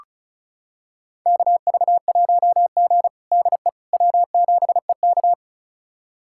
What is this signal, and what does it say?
Morse code, heard as a keyed tone, about 36 wpm, 700 Hz. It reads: KV1G DE W7EK
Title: USA Callsign DE Callsign - 4 Characters - 35wpm